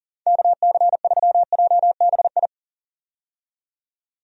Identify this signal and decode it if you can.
Morse code, heard as a keyed tone, about 40 wpm, 700 Hz. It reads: KC3JBI
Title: USA Call Signs - 6 Characters - 40wpm